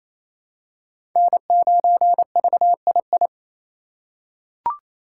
Morse code, heard as a keyed tone, about 27 wpm, 700 Hz. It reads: N9VII
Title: USA Call Signs - 4x Character Spacing - 28wpm